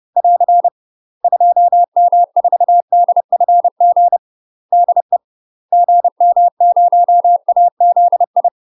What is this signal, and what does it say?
Morse code, heard as a keyed tone, about 30 wpm, 700 Hz. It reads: <AR> 2M4DFG DE GM0AZI
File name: UK MASTER QSOs TRAINING / 30WPM / UK MASTER QSOs 0029